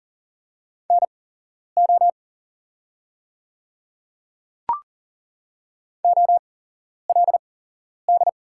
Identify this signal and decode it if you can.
Morse code, heard as a keyed tone, about 40 wpm, spaced as for 11 wpm, 700 Hz. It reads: NO OLD
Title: 8x Character Spacing - Top 300 Words - 40wpm